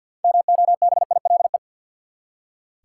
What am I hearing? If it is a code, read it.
Morse code, heard as a keyed tone, about 46 wpm, 700 Hz. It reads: MOBILE
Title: Six Letter Words - 50wpm